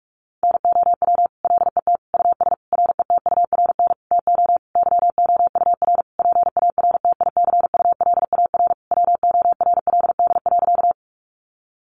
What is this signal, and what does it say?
Morse code, heard as a keyed tone, about 44 wpm, 700 Hz. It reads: NOW LET US RETURN TO YOUR PARTICULAR WORLD.
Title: Sentences from Top 800 Words 45wpm